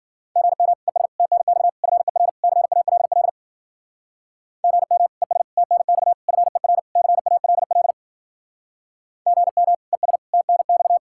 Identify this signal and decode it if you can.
Morse code, heard as a keyed tone, about 50 wpm, 700 Hz. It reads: GM ES TNX FER CALL GM ES TNX FER CALL GM ES TNX